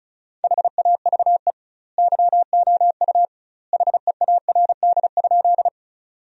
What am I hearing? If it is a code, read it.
Morse code, heard as a keyed tone, about 34 wpm, 700 Hz. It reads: HAVE YOU HEARD?
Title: 2x Word-Spacing - Sentences from Top 400 Words - 35wpm